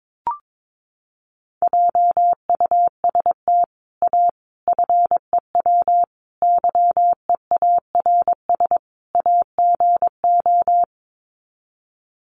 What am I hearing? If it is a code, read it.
Morse code, heard as a keyed tone, about 22 wpm, 700 Hz. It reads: JUST A FEW YEARS AGO
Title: Sets of 5 Words Encore 22wpm